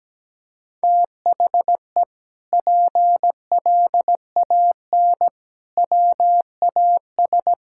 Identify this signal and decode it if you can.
Morse code, heard as a keyed tone, about 17 wpm, 700 Hz. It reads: THE PLAN WAS
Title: Sentences from Top 1000 Words - Review Part 6 - 17wpm